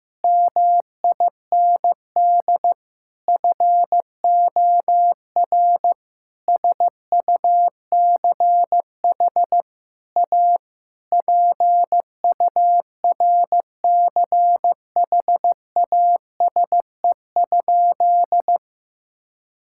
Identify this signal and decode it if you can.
Morse code, heard as a keyed tone, about 15 wpm, 700 Hz. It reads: MIND FOR SUCH A PURCHASE?
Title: Sentences from Top 1000 Words - Review Part 6 - 15wpm